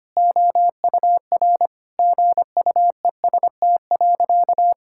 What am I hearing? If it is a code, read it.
Morse code, heard as a keyed tone, about 25 wpm, 700 Hz. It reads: OUR GUEST.
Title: Sentences from Top 1600 Words 25wpm